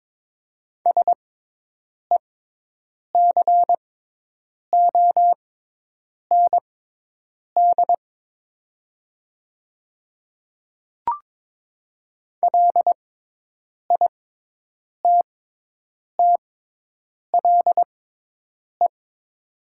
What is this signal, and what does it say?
Morse code, heard as a keyed tone, about 22 wpm, spaced as for 8 wpm, 700 Hz. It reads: SECOND LITTLE
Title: 6x Character Spacing - Top 300 Words - 22wpm